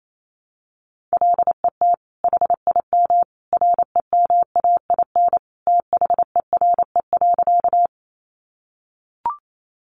Morse code, heard as a keyed tone, about 28 wpm, 700 Hz. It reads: LET HIM REMAIN THERE.
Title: Sentences from Top 1200 Words 28wpm